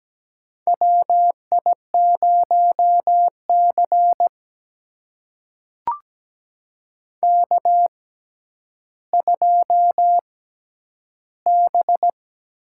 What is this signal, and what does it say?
Morse code, heard as a keyed tone, about 17 wpm, spaced as for 10 wpm, 700 Hz. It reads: WI0C K2B
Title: USA Call Signs - 4 Characters - 6x Character Spacing - 17wpm